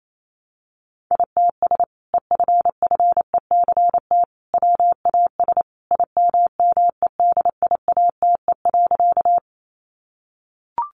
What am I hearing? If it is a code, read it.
Morse code, heard as a keyed tone, about 28 wpm, 700 Hz. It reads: ITS EFFECT WAS IMMEDIATE.
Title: Sentences from Top 1200 Words 28wpm